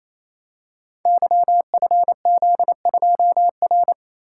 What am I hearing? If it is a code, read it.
Morse code, heard as a keyed tone, about 29 wpm, 700 Hz. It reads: YFZ2R
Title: Code Groups - 5 Letter:Numbers - 28wpm